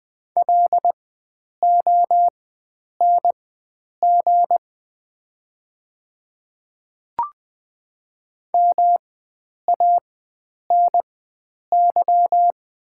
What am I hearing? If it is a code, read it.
Morse code, heard as a keyed tone, about 20 wpm, spaced as for 9 wpm, 700 Hz. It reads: LONG MANY
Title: 4x Character Spacing - Top 100 Words - 20wpm